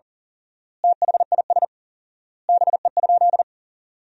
Morse code, heard as a keyed tone, about 37 wpm, 700 Hz. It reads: THIS BE?
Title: 4x Word-Spacing - Sentences from Top 100 Words - 40wpm